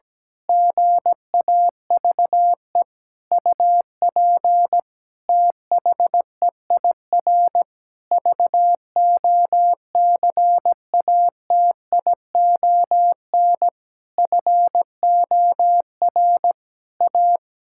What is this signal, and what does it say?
Morse code, heard as a keyed tone, about 17 wpm, 700 Hz. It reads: GAVE UP THEIR VOCATION FOR A
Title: Pangram Challenge v2 - 17wpm